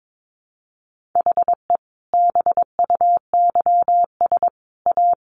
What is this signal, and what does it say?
Morse code, heard as a keyed tone, about 22 wpm, 700 Hz. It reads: HE BUYS A
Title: Easy Phrases 22wpm